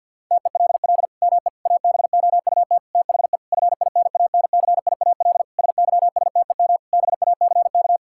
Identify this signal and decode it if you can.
Morse code, heard as a keyed tone, about 52 wpm, 700 Hz. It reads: TELL ME ABOUT THE FINANCIAL SYSTEM BACK
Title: Sentences from Top 1000 Words - Review Part 5 - 50wpm